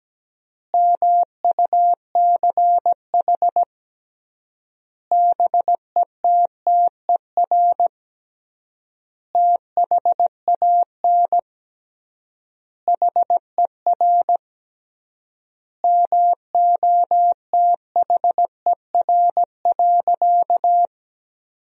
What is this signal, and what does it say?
Morse code, heard as a keyed tone, about 17 wpm, 700 Hz. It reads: MUCH BETTER THAN HER MOTHER.
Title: 3x Word-Spacing - Sentences from Top 400 Words - 17wpm